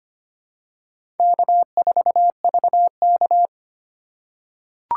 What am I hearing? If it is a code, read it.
Morse code, heard as a keyed tone, about 25 wpm, 700 Hz. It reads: K4VK
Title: USA Call Signs - 4 Characters - 4x Character Spacing - 25wpm